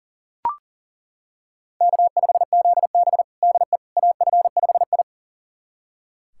Morse code, heard as a keyed tone, about 40 wpm, 700 Hz. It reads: K5ZB DE AF5I
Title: USA Callsign DE Callsign - 4 Characters - 40wpm